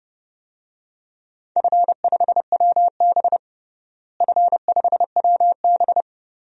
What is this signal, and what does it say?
Morse code, heard as a keyed tone, about 32 wpm, 700 Hz. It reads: F5WB F5WB
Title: DXCC-100 Call Signs - 4 Characters - Repeated 2x - 30wpm